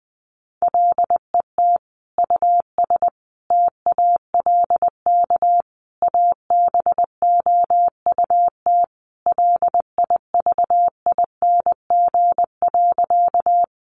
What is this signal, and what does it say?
Morse code, heard as a keyed tone, about 20 wpm, 700 Hz. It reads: LET US TALK ABOUT LIVING.
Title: Sentences from Top 1100 Words 20wpm